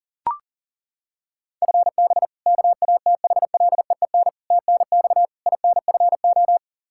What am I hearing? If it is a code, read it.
Morse code, heard as a keyed tone, about 37 wpm, 700 Hz. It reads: FB KATHLEEN TNX INFO
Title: Practice QSO - Exchange 3 - More Info - 40wpm